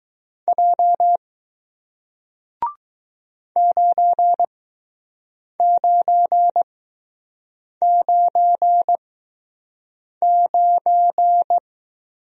Morse code, heard as a keyed tone, about 20 wpm, spaced as for 6 wpm, 700 Hz. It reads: J 9999